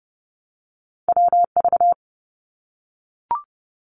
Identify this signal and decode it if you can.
Morse code, heard as a keyed tone, about 30 wpm, 700 Hz. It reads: WV
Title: US States Repeated 3x 30wpm